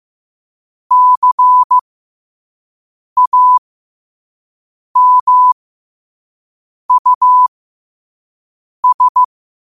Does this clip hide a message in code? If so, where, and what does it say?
Yes: Morse code from 0.9 to 9.2 s, heard as a keyed tone, about 15 wpm, spaced as for 5 wpm, 1000 Hz. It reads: CAMUS